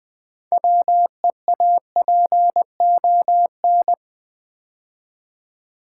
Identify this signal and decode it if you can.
Morse code, heard as a keyed tone, about 20 wpm, 700 Hz. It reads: WEAPON
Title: 6 Letter Words - First 3 Letters Given - 20wpm